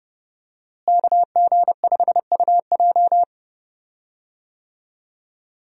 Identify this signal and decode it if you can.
Morse code, heard as a keyed tone, about 30 wpm, 700 Hz. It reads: KG5UJ
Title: USA Call Signs - 5 Characters - Repeated 3x - 30wpm